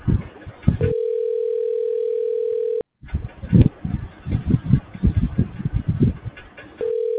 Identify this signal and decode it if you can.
Ringback tone (North America), 440 + 480 Hz.